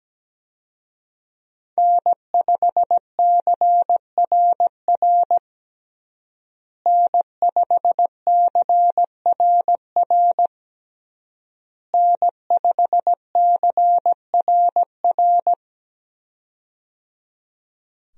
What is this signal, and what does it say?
Morse code, heard as a keyed tone, about 17 wpm, 700 Hz. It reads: N5CRR N5CRR N5CRR
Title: USA Call Signs - 5 Characters - Repeated 3x - 17wpm